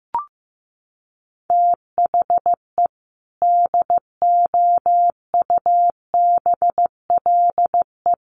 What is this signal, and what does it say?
Morse code, heard as a keyed tone, about 15 wpm, 700 Hz. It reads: THE DOUBLE
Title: Sentences from Top 1400 Words 15wpm